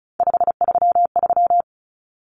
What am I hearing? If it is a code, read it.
Morse code, heard as a keyed tone, about 34 wpm, 700 Hz. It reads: 533
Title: Numbers - 3-Digit 35wpm